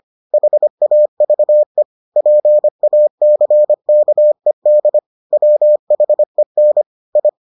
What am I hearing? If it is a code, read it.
Morse code, heard as a keyed tone, about 25 wpm, 600 Hz. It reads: HAVE PACKED WHEN I